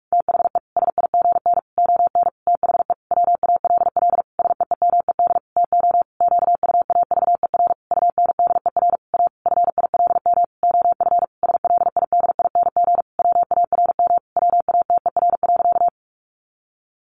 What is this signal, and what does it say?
Morse code, heard as a keyed tone, about 44 wpm, 700 Hz. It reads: THE SIGN ON THE WALL SEEMED TO QUAVER UNDER A FILM OF SLIDING WARM WATER.
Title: A Sound Of Thunder 45wpm